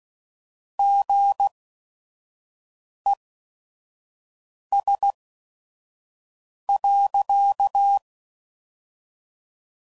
Morse code, heard as a keyed tone, about 16 wpm, 800 Hz. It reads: G E S .